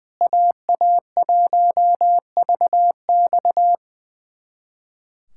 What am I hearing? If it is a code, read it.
Morse code, heard as a keyed tone, about 20 wpm, 700 Hz. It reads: AA1VX
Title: USA Callsign DE Callsign - 5 Characters - 20wpm